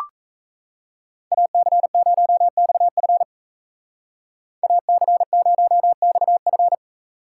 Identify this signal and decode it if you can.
Morse code, heard as a keyed tone, about 33 wpm, 700 Hz. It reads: AC0XF AC0XF
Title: Speed Racing - USA 2x2 Call Signs 28wpm